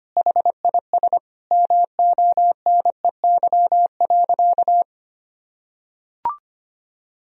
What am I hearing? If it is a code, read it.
Morse code, heard as a keyed tone, about 25 wpm, 700 Hz. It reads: HIS MONEY.
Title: Sentences from Top 1000 Words - Review Part 4 - 25wpm